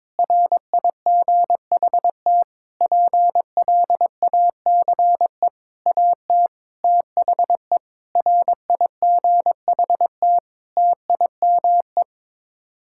Morse code, heard as a keyed tone, about 22 wpm, 700 Hz. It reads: RIGHT PLACE AT THE RIGHT TIME
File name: Sets of 9 Words - 22wpm